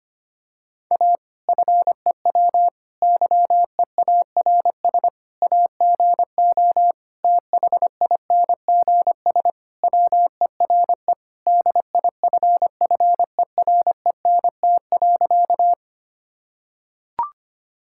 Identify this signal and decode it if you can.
Morse code, heard as a keyed tone, about 25 wpm, 700 Hz. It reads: A FEW YEARS AGO THINGS WERE DIFFERENT.
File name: practice / 25wpm / Sentences with 4-Word Phrases - 25wpm